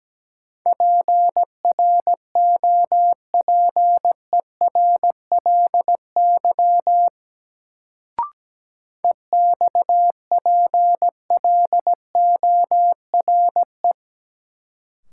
Morse code, heard as a keyed tone, about 17 wpm, 700 Hz. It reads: PROPERLY EXPLORE
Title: Sentences from Top 1100 Words - New Word Drill - 17wpm